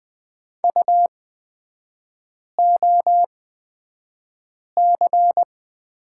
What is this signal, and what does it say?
Morse code, heard as a keyed tone, about 20 wpm, spaced as for 5 wpm, 700 Hz. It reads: UOC
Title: 243 Lesson 39e - Letter z - Three Characters 20wpm